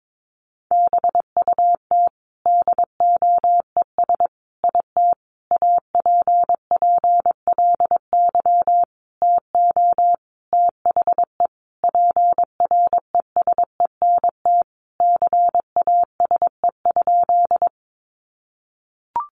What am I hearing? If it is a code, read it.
Morse code, heard as a keyed tone, about 22 wpm, 700 Hz. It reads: BUT DOES IT APPLY TO THE PRESENT CASE?
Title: Sentences from Top 1200 Words 22wpm